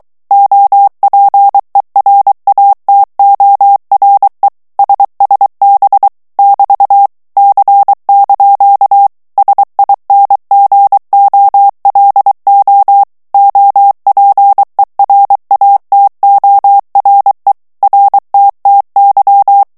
Morse code, heard as a keyed tone, about 23 wpm, 800 Hz. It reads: OPERATORE SSB <BT> C) SINGOLO OPERATORE RTTY
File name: ITALIEN texte clair 2 vitesse 22 mots minute